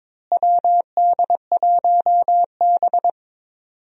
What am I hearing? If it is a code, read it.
Morse code, heard as a keyed tone, about 22 wpm, 700 Hz. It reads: WD1B
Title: USA Call Signs - 4 Characters - Repeated 2x - 22wpm